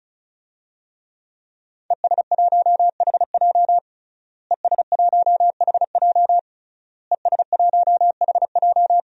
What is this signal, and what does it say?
Morse code, heard as a keyed tone, about 36 wpm, 700 Hz. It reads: ES1HJ ES1HJ ES1HJ
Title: DXCC-100 Call Signs - 5 Characters - Repeated 3x - 35wpm